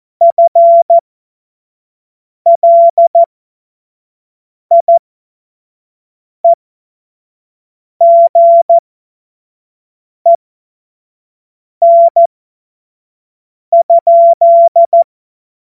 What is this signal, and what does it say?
Morse code, heard as a keyed tone, about 14 wpm, spaced as for 5 wpm, 680 Hz. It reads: FLIEGEN?
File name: Morsekurs / Witze / Witz 07. Februar 2023 25 F70